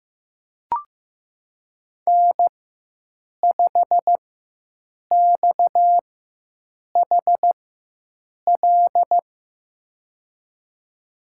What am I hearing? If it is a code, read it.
Morse code, heard as a keyed tone, about 15 wpm, spaced as for 7 wpm, 700 Hz. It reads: N5XHL
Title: USA Call Signs - 5 Characters - 4x Character Spacing - 15wpm